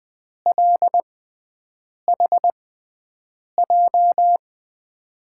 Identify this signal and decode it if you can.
Morse code, heard as a keyed tone, about 21 wpm, spaced as for 7 wpm, 700 Hz. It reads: LHJ
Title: USA Call Signs - 5 Characters - 6x Character Spacing - 20wpm